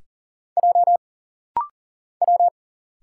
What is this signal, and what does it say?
Morse code, heard as a keyed tone, about 40 wpm, 700 Hz. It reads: J W
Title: Single Letters - Rapid-Fire 40wpm